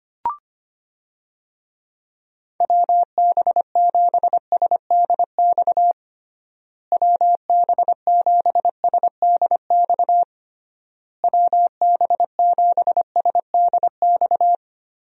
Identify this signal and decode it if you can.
Morse code, heard as a keyed tone, about 25 wpm, 700 Hz. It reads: WB7SDX WB7SDX WB7SDX
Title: USA Call Signs - 6 Characters - Repeated 3x - 25wpm